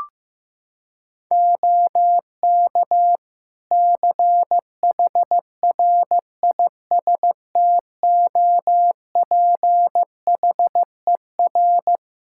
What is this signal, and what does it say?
Morse code, heard as a keyed tone, about 15 wpm, 700 Hz. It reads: OK CHRISTOPHER
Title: Practice QSO - Exchange 4 - Ending - 15wpm